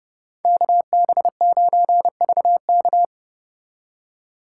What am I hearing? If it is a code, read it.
Morse code, heard as a keyed tone, about 33 wpm, 700 Hz. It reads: KB9VK